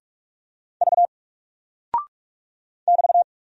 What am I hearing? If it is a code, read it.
Morse code, heard as a keyed tone, about 44 wpm, 700 Hz. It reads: V <BT>
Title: Single Character (Letter:Number:Punctuation:Prosign) Alt 45wpm